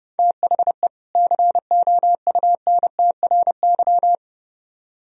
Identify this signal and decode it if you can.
Morse code, heard as a keyed tone, about 35 wpm, 700 Hz. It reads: THE COUNTRY
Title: Speed-Racing - Sets of 2 Words 30wpm